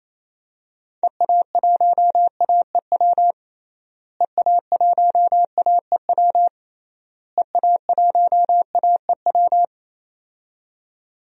Morse code, heard as a keyed tone, about 28 wpm, 700 Hz. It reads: EA1AEW EA1AEW EA1AEW
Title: DXCC-100 Call Signs - 6 Characters - Repeated 3x - 28wpm